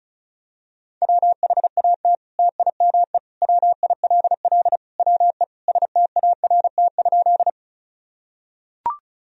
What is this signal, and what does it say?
Morse code, heard as a keyed tone, about 34 wpm, 700 Hz. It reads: WHAT TIME WILL WE START?
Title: Sentences from Top 1000 Words - Review Part 1 - 35wpm